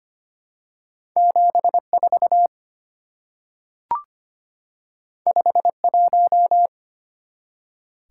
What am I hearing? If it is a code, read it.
Morse code, heard as a keyed tone, about 25 wpm, 700 Hz. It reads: 74 51
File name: Two Letter-Number Combinations - 25wpm